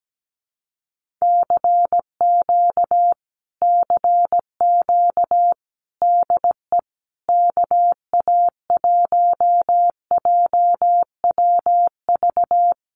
Morse code, heard as a keyed tone, about 17 wpm, 700 Hz. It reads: CQ CQ DE KA1JWV
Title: Common QSO Phrases Encore 17wpm